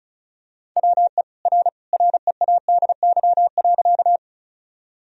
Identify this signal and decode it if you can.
Morse code, heard as a keyed tone, about 36 wpm, 700 Hz. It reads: WE R READY.
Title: Sentences from Top 400 Words and Abbreviations 35wpm